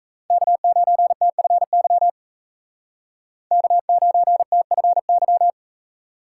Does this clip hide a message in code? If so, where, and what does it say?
Yes: Morse code from 0.3 to 5.5 s, heard as a keyed tone, about 35 wpm, 700 Hz. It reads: K9TFY K9TFY